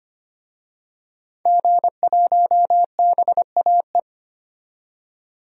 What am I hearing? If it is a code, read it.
Morse code, heard as a keyed tone, about 25 wpm, 700 Hz. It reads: G1BAE
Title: DXCC-100 Call Signs - 5 Characters - 25wpm